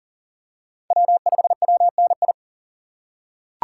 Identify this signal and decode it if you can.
Morse code, heard as a keyed tone, about 40 wpm, 700 Hz. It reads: W5WNI
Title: USA Call Signs - 5 Characters - 3x Character Spacing - 40wpm